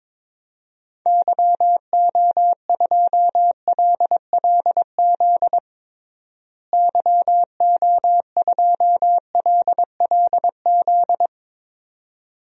Morse code, heard as a keyed tone, about 21 wpm, 700 Hz. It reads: YO2LLZ YO2LLZ
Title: DXCC-100 Call Signs - 6 Characters - Repeated 2x - 22wpm